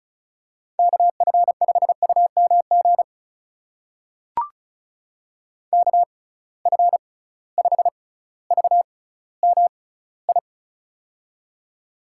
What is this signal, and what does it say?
Morse code, heard as a keyed tone, about 36 wpm, spaced as for 18 wpm, 700 Hz. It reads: KF5UMG KF5VMI
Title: USA Call Signs - 6 Characters - 6x Character Spacing - 35wpm